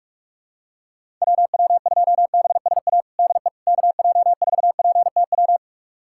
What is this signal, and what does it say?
Morse code, heard as a keyed tone, about 47 wpm, 700 Hz. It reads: WW2BSA DE KJ4PTW
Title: USA Callsign DE Callsign - 6 Characters - 45wpm